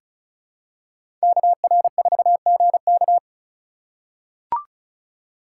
Morse code, heard as a keyed tone, about 36 wpm, 700 Hz. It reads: KR4GK
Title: USA Call Signs - 2x Character Spacing - 35wpm